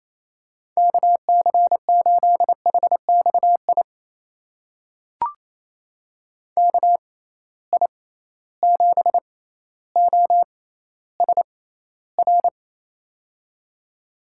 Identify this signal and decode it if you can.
Morse code, heard as a keyed tone, about 28 wpm, spaced as for 14 wpm, 700 Hz. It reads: KC8HXI KI7OSR
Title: USA Call Signs - 6 Characters - 6x Character Spacing - 28wpm